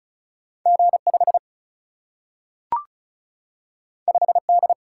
Morse code, heard as a keyed tone, about 36 wpm, 700 Hz. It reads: G5 5D
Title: Single-Letter Single-Number 35wpm